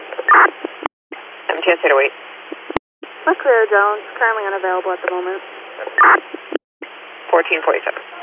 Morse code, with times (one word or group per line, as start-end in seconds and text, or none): none